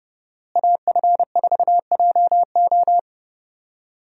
Morse code, heard as a keyed tone, about 32 wpm, 700 Hz. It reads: AF4JO
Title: USA Call Signs - 5 Characters - Repeated 3x - 30wpm